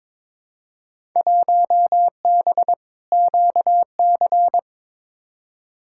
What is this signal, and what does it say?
Morse code, heard as a keyed tone, about 22 wpm, 700 Hz. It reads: 1B QC
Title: Field Day Exchanges Encore - 22wpm